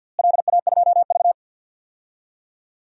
Morse code, heard as a keyed tone, about 46 wpm, 700 Hz. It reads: LA2V
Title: DXCC-100 Call Signs - 4 Characters - Repeated 3x - 50wpm